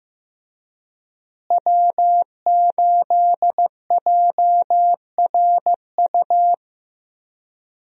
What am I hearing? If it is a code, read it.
Morse code, heard as a keyed tone, about 14 wpm, 700 Hz. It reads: W8JRU
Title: USA Call Signs - 5 Characters - Repeated 3x - 15wpm